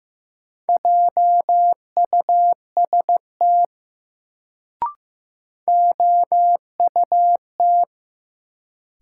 Morse code, heard as a keyed tone, about 15 wpm, 700 Hz. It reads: JUST OUT
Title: Sets of 2 Words - New Word Drill - 15wpm